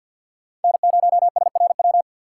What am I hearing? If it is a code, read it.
Morse code, heard as a keyed tone, about 49 wpm, 700 Hz. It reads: N0SRW
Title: USA Call Signs - 5 Characters - 3x Character Spacing - 50wpm